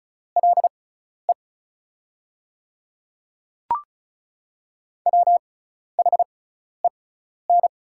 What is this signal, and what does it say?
Morse code, heard as a keyed tone, about 35 wpm, spaced as for 12 wpm, 690 Hz. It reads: LE WHEN
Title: 6x Character Spacing - Top 200 Words - 35wpm